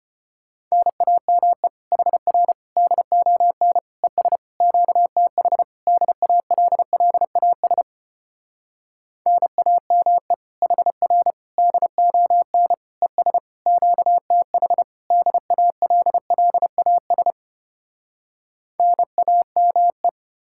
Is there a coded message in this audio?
Yes: Morse code from 0.7 to 20.1 s, heard as a keyed tone, about 33 wpm, 700 Hz. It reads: NAME HR DON ES QTH DALLAS NAME HR DON ES QTH DALLAS NAME